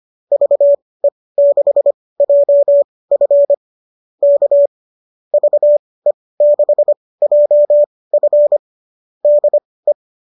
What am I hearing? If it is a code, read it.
Morse code, heard as a keyed tone, about 25 wpm, spaced as for 18 wpm, 590 Hz. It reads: VE6JF K VE6JF DE